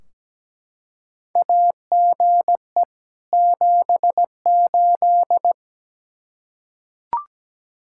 Morse code, heard as a keyed tone, about 17 wpm, 700 Hz. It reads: AGE 78
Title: Practice QSO - Exchange 3 - More Info - 17wpm